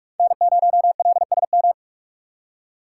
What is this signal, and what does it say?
Morse code, heard as a keyed tone, about 46 wpm, 700 Hz. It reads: N0LSM
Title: USA Call Signs - 3x Character Spacing - 45wpm